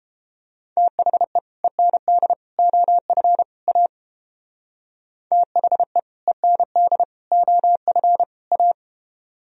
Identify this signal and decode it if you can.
Morse code, heard as a keyed tone, about 29 wpm, 700 Hz. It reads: THE END OF A THE END OF A